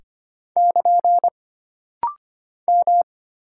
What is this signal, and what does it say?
Morse code, heard as a keyed tone, about 25 wpm, 700 Hz. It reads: <KN> M
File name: Single Character (Letter:Number:Punctuation:Prosign) - Rapid-Fire - 25wpm